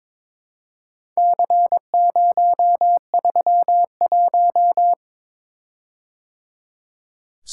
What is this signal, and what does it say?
Morse code, heard as a keyed tone, about 22 wpm, 690 Hz. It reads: C031
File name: practice / 22wpm / Four Letter-Number Combinations - 22wpm